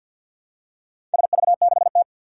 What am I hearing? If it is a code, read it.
Morse code, heard as a keyed tone, about 47 wpm, 700 Hz. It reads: SV6T
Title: Four Challenging Letter-Number Combinations 50wpm